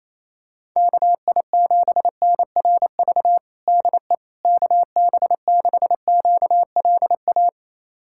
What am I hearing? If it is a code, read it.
Morse code, heard as a keyed tone, about 27 wpm, 700 Hz. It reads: KI7NRV DE KB6QLA
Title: USA Callsign DE Callsign 28wpm